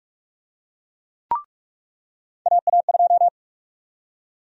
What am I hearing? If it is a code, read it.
Morse code, heard as a keyed tone, about 42 wpm, 700 Hz. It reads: AA2
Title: USA Call Sign Prefixes - 45wpm